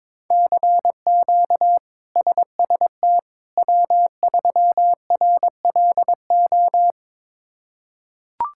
Morse code, heard as a keyed tone, about 22 wpm, 700 Hz. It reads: CQ SST W3RLO
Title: K1USN SST QSO Practice - 22wpm